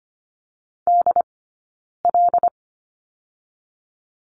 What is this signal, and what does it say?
Morse code, heard as a keyed tone, about 25 wpm, 700 Hz. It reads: D L
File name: course / 25:10wpm / 092 Lesson 16d - Letter c - Two Characters 25wpm